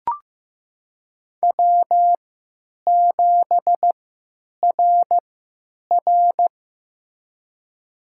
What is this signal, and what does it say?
Morse code, heard as a keyed tone, about 15 wpm, spaced as for 9 wpm, 700 Hz. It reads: W7RR
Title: USA Call Signs - 4 Characters - 3x Character Spacing - 15wpm